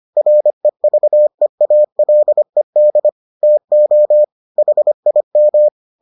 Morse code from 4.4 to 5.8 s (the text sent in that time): HIM